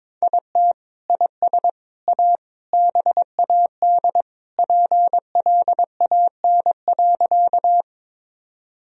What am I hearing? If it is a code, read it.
Morse code, heard as a keyed tone, about 22 wpm, 700 Hz. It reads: IT IS A BAD PLAN.
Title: Sentences from Top 1000 Words - Review Part 2 - 22wpm